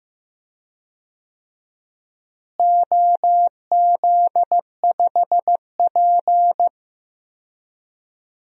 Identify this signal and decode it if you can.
Morse code, heard as a keyed tone, about 15 wpm, 700 Hz. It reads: OZ5P
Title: DXCC-100 Call Signs - 4 Characters - Repeated 3x - 15wpm